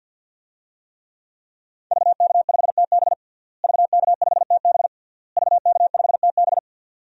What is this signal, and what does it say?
Morse code, heard as a keyed tone, about 48 wpm, 700 Hz. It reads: VK5TB VK5TB VK5TB
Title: DXCC-100 Call Signs - 5 Characters - Repeated 3x - 50wpm